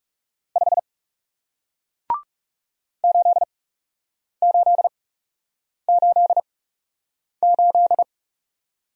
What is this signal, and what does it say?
Morse code, heard as a keyed tone, about 41 wpm, spaced as for 8 wpm, 700 Hz. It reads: 5 8888